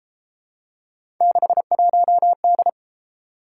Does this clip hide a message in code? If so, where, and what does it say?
Yes: Morse code from 1.2 to 2.7 s, heard as a keyed tone, about 31 wpm, 700 Hz. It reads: 61D